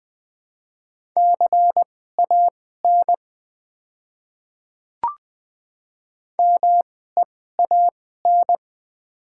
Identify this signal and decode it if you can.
Morse code, heard as a keyed tone, about 20 wpm, spaced as for 14 wpm, 700 Hz. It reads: CAN MEAN
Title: 2x Character Spacing - Top 200 Words - 20wpm